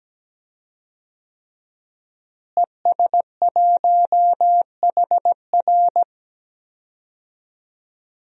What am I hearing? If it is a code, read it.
Morse code, heard as a keyed tone, about 17 wpm, 700 Hz. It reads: ES1HR
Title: DXCC-100 Call Signs - 5 Characters - Repeated 3x - 17wpm